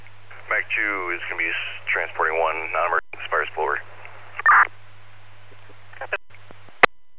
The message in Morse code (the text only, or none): none